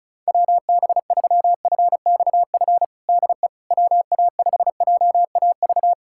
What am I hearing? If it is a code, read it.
Morse code, heard as a keyed tone, about 34 wpm, 700 Hz. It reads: WB3FXF DE WA5JAV
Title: USA Callsign DE Callsign 35wpm